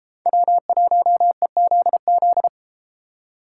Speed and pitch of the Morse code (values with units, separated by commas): 32 wpm, 700 Hz